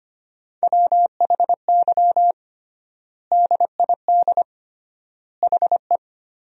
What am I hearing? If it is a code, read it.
Morse code, heard as a keyed tone, about 25 wpm, 700 Hz. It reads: WHY DID HE
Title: 3x Word-Spacing - Sentences from Top 200 Words - 25wpm